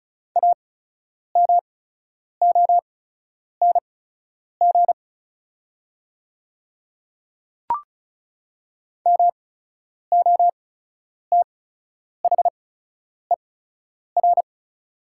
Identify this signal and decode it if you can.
Morse code, heard as a keyed tone, about 36 wpm, spaced as for 10 wpm, 690 Hz. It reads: AMONG MOTHER
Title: 8x Character Spacing - Top 500 Words - 35wpm